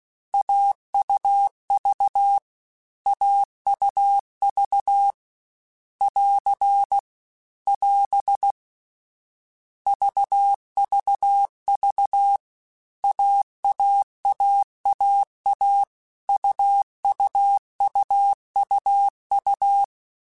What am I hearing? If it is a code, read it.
Morse code, heard as a keyed tone, about 16 wpm, 800 Hz. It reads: AUV AUV <AR> <AS> VVV AAAAA UUUUU